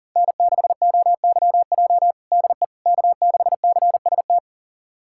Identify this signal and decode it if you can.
Morse code, heard as a keyed tone, about 37 wpm, 700 Hz. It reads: N6OYJ DE K6CST
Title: USA Callsign DE Callsign - 5 Characters - 40wpm